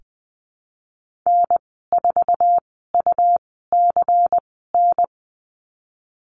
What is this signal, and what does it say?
Morse code, heard as a keyed tone, about 20 wpm, spaced as for 15 wpm, 700 Hz. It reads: N4UCN
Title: 128 Lesson 21g - Number 6 - Call Signs 20wpm